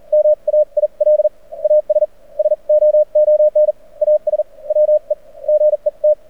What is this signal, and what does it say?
Morse code, heard as a keyed tone, about 44 wpm, spaced as for 37 wpm, 590 Hz. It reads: MAIL AS SOON AS WE GET
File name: archived / CRSnet